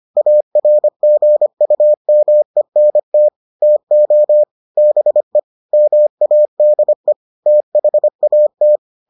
Morse code, heard as a keyed tone, about 25 wpm, 600 Hz. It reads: ARGUMENT TO BE MADE THAT